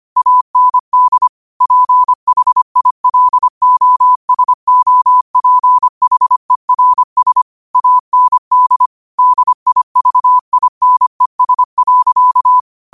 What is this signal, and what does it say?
Morse code, heard as a keyed tone, about 25 wpm, 1000 Hz. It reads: AND PHILOSOPHERS AND DIVINES.